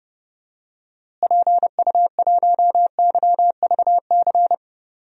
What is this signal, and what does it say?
Morse code, heard as a keyed tone, about 30 wpm, 700 Hz. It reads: PU1YVC
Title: DXCC-100 Call Signs - 6 Characters - Repeated 3x - 30wpm